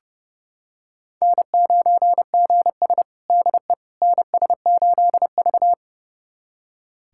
Morse code, heard as a keyed tone, about 30 wpm, 700 Hz. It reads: N9GS DE NS8V
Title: USA Callsign DE Callsign - 4 Characters - 30wpm